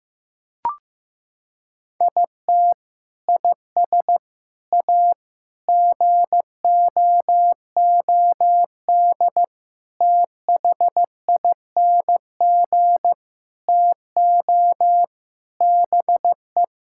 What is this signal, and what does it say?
Morse code, heard as a keyed tone, about 15 wpm, 700 Hz. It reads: IT IS A GOOD THING TO BE
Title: Sentences from Top 1000 Words - Review Part 5 - 15wpm